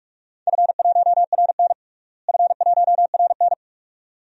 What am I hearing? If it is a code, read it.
Morse code, heard as a keyed tone, about 41 wpm, 700 Hz. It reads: F1RN F1RN
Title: DXCC-100 Call Signs - 4 Characters - Repeated 2x - 45wpm